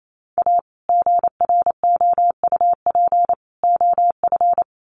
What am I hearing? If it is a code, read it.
Morse code, heard as a keyed tone, about 28 wpm, 700 Hz. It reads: A GROUP OF
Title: Sets of 3 Words 28wpm